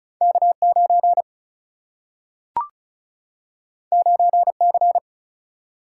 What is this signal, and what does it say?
Morse code, heard as a keyed tone, about 36 wpm, 700 Hz. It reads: K 9 9 C
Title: Single-Letter Single-Number 35wpm